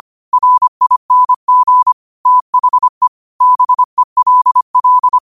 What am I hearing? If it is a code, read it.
Morse code, heard as a keyed tone, about 25 wpm, 1000 Hz. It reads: RING THE BELL
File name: qotdfiles / 25wpm